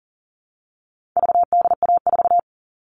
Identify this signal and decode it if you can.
Morse code, heard as a keyed tone, about 40 wpm, 700 Hz. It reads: VDA4
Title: All Cut Numbers - 4-Digit 40wpm